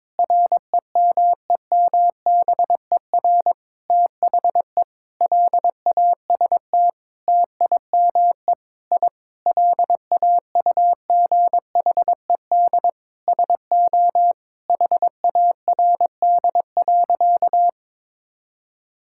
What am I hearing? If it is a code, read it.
Morse code, heard as a keyed tone, about 22 wpm, 700 Hz. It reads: REMEMBER THE LAST TIME I LAUGHED SO HARD.
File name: Sentences with 7-Word Phrases - 22wpm